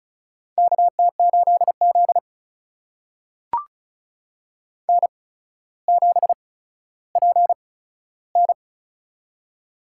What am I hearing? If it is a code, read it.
Morse code, heard as a keyed tone, about 36 wpm, spaced as for 15 wpm, 700 Hz. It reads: KT8Z N7PN
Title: USA Call Signs - 4 Characters - 8x Character Spacing - 35wpm